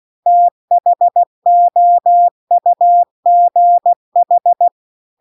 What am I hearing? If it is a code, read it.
Morse code, heard as a keyed tone, about 16 wpm, 700 Hz. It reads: THOUGH